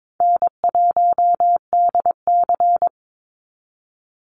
Morse code, heard as a keyed tone, about 22 wpm, 700 Hz. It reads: N1DC
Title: Easy Call Signs Repeated 2x - 22wpm